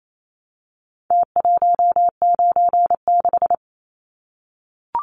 Morse code, heard as a keyed tone, about 28 wpm, 700 Hz. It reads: T196
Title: Numbers - Contest Style - 4-Digit 28wpm